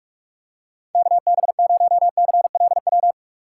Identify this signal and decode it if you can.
Morse code, heard as a keyed tone, about 48 wpm, 700 Hz. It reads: KB0CLW
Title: USA Call Signs - 3x Character Spacing - 45wpm